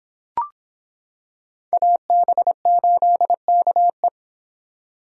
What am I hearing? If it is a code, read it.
Morse code, heard as a keyed tone, about 26 wpm, 700 Hz. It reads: AB8KE